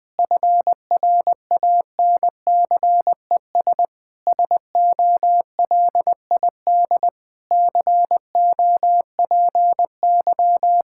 Morse code, heard as a keyed tone, about 21 wpm, 700 Hz. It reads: FRANCES SOLID COPY
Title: Practice QSO - Exchange 3 - More Info - 20wpm